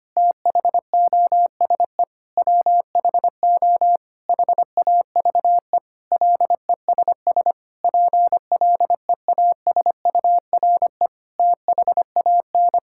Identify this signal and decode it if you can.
Morse code, heard as a keyed tone, about 25 wpm, 700 Hz. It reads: THOSE WHO HAVE LESS PLEASURE THAN
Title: Sentences from Top 1600 Words 25wpm